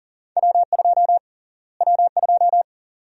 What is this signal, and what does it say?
Morse code, heard as a keyed tone, about 37 wpm, 700 Hz. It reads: W2 W2
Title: USA Call Sign Prefixes - Repeated 2x - 40wpm